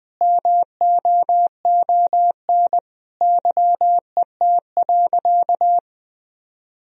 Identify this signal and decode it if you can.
Morse code, heard as a keyed tone, about 20 wpm, 700 Hz. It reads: MOON YET.
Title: Sentences from Top 1000 Words - Review Part 3 - 20wpm